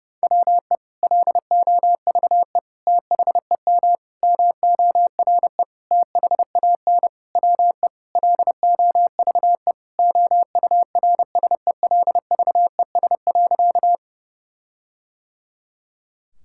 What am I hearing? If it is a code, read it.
Morse code, heard as a keyed tone, about 32 wpm, 700 Hz. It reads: WE LOVE THEM MORE THAN WE LOVE OURSELVES.
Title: Sentences from Top 1600 Words 30wpm